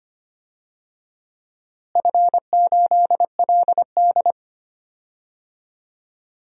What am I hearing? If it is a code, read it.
Morse code, heard as a keyed tone, about 25 wpm, 700 Hz. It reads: F8LD
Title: DXCC-100 Call Signs - 4 Characters - Repeated 2x - 25wpm